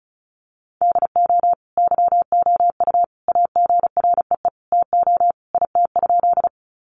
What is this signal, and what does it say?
Morse code, heard as a keyed tone, about 34 wpm, 700 Hz. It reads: DO YOU AGREE TO IT?
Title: Sentences from Top 1100 Words 35wpm